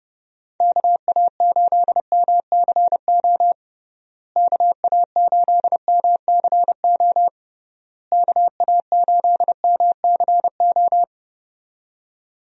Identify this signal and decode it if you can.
Morse code, heard as a keyed tone, about 30 wpm, 700 Hz. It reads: KA8MCO KA8MCO KA8MCO
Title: USA Call Signs - 6 Characters - Repeated 3x - 30wpm